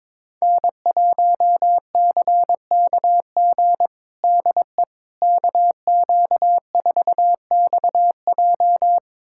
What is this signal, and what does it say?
Morse code, heard as a keyed tone, about 22 wpm, 700 Hz. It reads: N1CKG DE KQ4XJ
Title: USA Callsign DE Callsign - 5 Characters - 22wpm